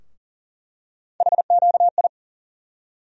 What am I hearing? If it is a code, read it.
Morse code, heard as a keyed tone, about 38 wpm, 700 Hz. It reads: HQI
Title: USA Call Sign Postfixes - 3 Letters 40wpm